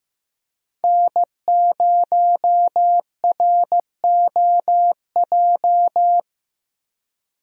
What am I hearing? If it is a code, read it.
Morse code, heard as a keyed tone, about 15 wpm, 700 Hz. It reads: N0ROJ
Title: USA Call Signs - 5 Characters - Repeated 3x - 15wpm